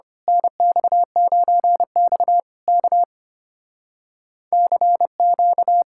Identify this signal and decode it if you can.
Morse code, heard as a keyed tone, about 24 wpm, 700 Hz. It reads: NX9X K CQ
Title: Speed-Racing Common QSO Phrases - 25wpm